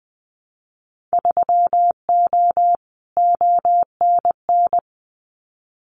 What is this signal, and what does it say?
Morse code, heard as a keyed tone, about 20 wpm, 700 Hz. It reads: 3O ONN
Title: Winter Field Day Exchanges 20wpm